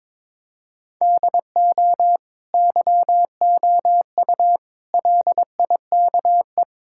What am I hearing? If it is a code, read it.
Morse code, heard as a keyed tone, about 21 wpm, 700 Hz. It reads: DO YOU LIKE
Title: Sentences from Top 1000 Words - Review Part 5 - 22wpm